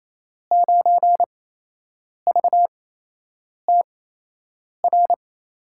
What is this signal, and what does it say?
Morse code, heard as a keyed tone, about 28 wpm, spaced as for 8 wpm, 700 Hz. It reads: 9VTR
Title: USA Call Signs - 5 Characters - 8x Character Spacing - 28wpm